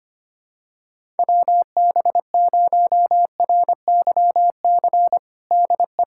Morse code, heard as a keyed tone, about 25 wpm, 700 Hz. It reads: WB0RYC DE
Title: USA Callsign DE Callsign 25wpm